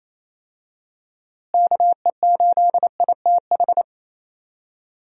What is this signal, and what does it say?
Morse code, heard as a keyed tone, about 28 wpm, 700 Hz. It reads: KE8ITH
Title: USA Call Signs - 6 Characters - Repeated 2x - 28wpm